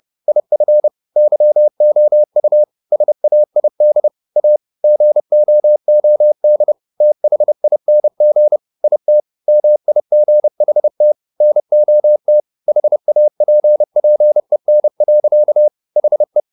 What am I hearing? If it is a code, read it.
Morse code, heard as a keyed tone, about 30 wpm, 600 Hz. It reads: IF YOU SAID A GOOD THING IT MIGHT NOT HAPPEN. HE
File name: the old man and the sea / the old man and the sea 1 30wpm